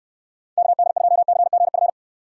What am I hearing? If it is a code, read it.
Morse code, heard as a keyed tone, about 67 wpm, 700 Hz. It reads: KD2CGF